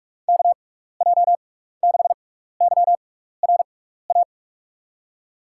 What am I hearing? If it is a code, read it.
Morse code, heard as a keyed tone, about 47 wpm, spaced as for 16 wpm, 700 Hz. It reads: KJ6YRA
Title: USA Call Signs - 6 Characters - 6x Character Spacing - 45wpm